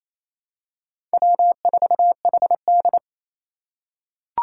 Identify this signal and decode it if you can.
Morse code, heard as a keyed tone, about 28 wpm, 700 Hz. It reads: W4HD
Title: Four Challenging Letter-Number Combinations 28wpm